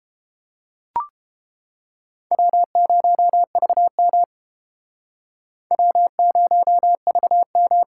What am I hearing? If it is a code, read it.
Morse code, heard as a keyed tone, about 29 wpm, 700 Hz. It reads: W0VM W0VM